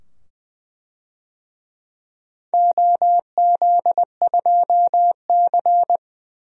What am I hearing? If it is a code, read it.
Morse code, heard as a keyed tone, about 20 wpm, 700 Hz. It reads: OZ2C
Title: DXCC-100 Call Signs - 4 Characters - 20wpm